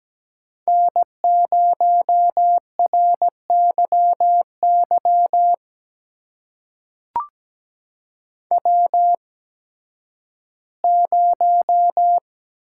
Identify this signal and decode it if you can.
Morse code, heard as a keyed tone, about 17 wpm, spaced as for 11 wpm, 700 Hz. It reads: N0RYY W0
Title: USA Call Signs - 5 Characters - 8x Character Spacing - 17wpm